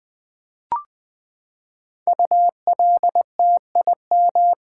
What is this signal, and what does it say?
Morse code, heard as a keyed tone, about 20 wpm, 700 Hz. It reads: ULTIM
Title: Build A Word - 5 Letters At A Time - 10-Letter Words - 20wpm